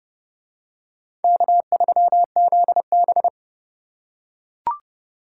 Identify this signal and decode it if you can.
Morse code, heard as a keyed tone, about 26 wpm, 700 Hz. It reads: K3ZB